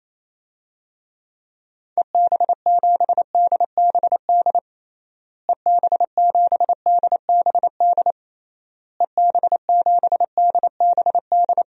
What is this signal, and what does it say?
Morse code, heard as a keyed tone, about 28 wpm, 700 Hz. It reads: EB7DBD EB7DBD EB7DBD
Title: DXCC-100 Call Signs - 6 Characters - Repeated 3x - 28wpm